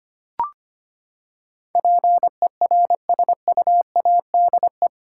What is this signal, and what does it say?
Morse code, heard as a keyed tone, about 25 wpm, 700 Hz. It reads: PERSUADE
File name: Eight Letter Words - 25wpm